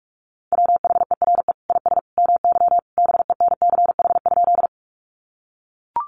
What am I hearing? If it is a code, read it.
Morse code, heard as a keyed tone, about 44 wpm, 700 Hz. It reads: WHERE IS MY BENCH?
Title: Sentences from Top 1400 Words 45wpm